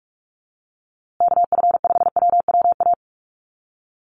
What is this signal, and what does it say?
Morse code, heard as a keyed tone, about 44 wpm, 700 Hz. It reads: KF5WWA
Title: USA Call Signs Repeated 3x 45wpm